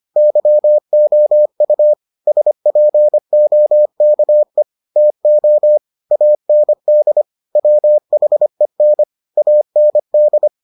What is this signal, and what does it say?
Morse code, heard as a keyed tone, about 25 wpm, 600 Hz. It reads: YOU SPOKE TO AND WHEN AND